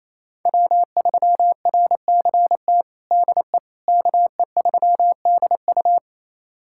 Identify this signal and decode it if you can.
Morse code, heard as a keyed tone, about 28 wpm, 700 Hz. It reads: W3RCT DE KE3DU
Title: USA Callsign DE Callsign - 5 Characters - 28wpm